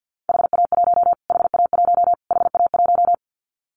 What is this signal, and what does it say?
Morse code, heard as a keyed tone, about 49 wpm, 700 Hz. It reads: HA1 HA1 HA1
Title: Callsign Prefixes for Top 25 DXCC - Repeated 3x - 50wpm